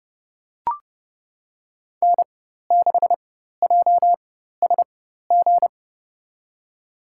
Morse code, heard as a keyed tone, about 30 wpm, spaced as for 14 wpm, 700 Hz. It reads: N6JSG
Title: USA Call Signs - 4x Character Spacing - 30wpm